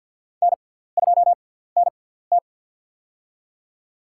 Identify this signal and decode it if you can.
Morse code, heard as a keyed tone, about 54 wpm, spaced as for 17 wpm, 700 Hz. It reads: N2NT
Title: USA Call Signs - 4 Characters - 6x Character Spacing - 50wpm